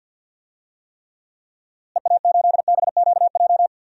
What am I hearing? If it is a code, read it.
Morse code, heard as a keyed tone, about 49 wpm, 700 Hz. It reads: EA8BQJ
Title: DXCC-100 Call Signs - 6 Characters - Repeated 2x - 50wpm